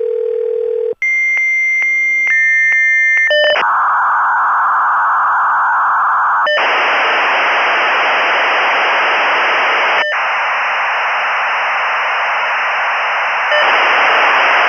Dial-up modem handshake.